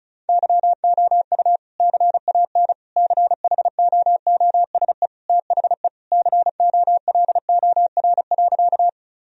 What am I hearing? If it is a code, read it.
Morse code, heard as a keyed tone, about 34 wpm, 700 Hz. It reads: YOU CAN CHOOSE THE COLOR.
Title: Sentences from Top 1000 Words - Review Part 4 - 35wpm